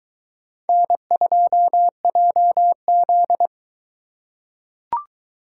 Morse code, heard as a keyed tone, about 21 wpm, 700 Hz. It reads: N2JZ